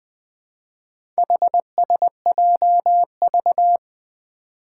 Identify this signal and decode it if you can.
Morse code, heard as a keyed tone, about 20 wpm, 700 Hz. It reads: HSJV
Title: Four Challenging Letter-Number Combinations 20wpm